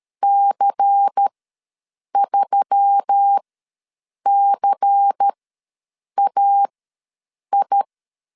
Morse code, heard as a keyed tone, about 13 wpm, spaced as for 7 wpm, 800 Hz. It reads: C3CAI
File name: LEZIONI / 5 LBVC / 5 LBVC ws 12wpm 3159